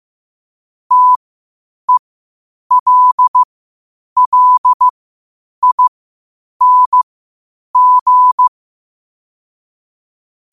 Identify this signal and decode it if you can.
Morse code, heard as a keyed tone, about 15 wpm, spaced as for 8 wpm, 1000 Hz. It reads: TELLING
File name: qotdfiles / 8wpm